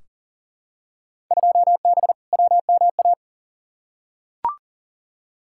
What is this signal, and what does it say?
Morse code, heard as a keyed tone, about 40 wpm, 700 Hz. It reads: 2B WMA
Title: Field Day Exchanges Encore - 40wpm